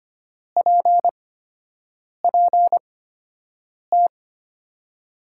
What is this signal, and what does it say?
Morse code, heard as a keyed tone, about 25 wpm, spaced as for 7 wpm, 700 Hz. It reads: PPT
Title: USA Call Signs - 6 Characters - 8x Character Spacing - 25wpm